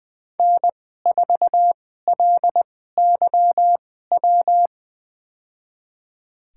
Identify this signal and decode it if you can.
Morse code, heard as a keyed tone, about 20 wpm, spaced as for 14 wpm, 700 Hz. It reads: N4LYW
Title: USA Call Signs - 5 Characters - 2x Character Spacing - 20wpm